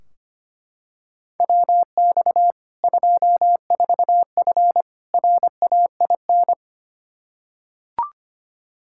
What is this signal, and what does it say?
Morse code, heard as a keyed tone, about 25 wpm, 700 Hz. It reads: WX 24F RAIN
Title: Practice QSO - Exchange 2 - Info - 25wpm